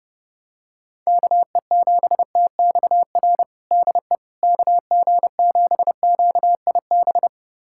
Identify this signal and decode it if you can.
Morse code, heard as a keyed tone, about 30 wpm, 700 Hz. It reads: KE7TXR DE KG7QIB
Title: USA Callsign DE Callsign - 6 Characters - 30wpm